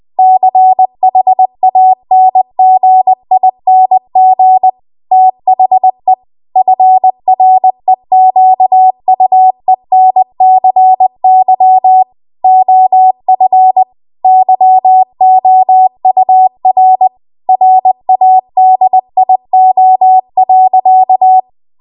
Morse code, heard as a keyed tone, about 20 wpm, 750 Hz. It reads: CHANGING THE FREQUENCY OF YOUR RADIO.
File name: Archive / 20 WPM / 20WPM